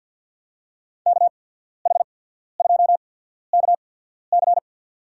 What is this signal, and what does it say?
Morse code, heard as a keyed tone, about 52 wpm, spaced as for 14 wpm, 700 Hz. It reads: KH2KC
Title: USA Call Signs - 5 Characters - 8x Character Spacing - 50wpm